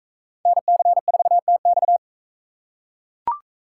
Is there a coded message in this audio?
Yes: Morse code from 0.5 to 2.0 s, heard as a keyed tone, about 40 wpm, 700 Hz. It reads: NC4TX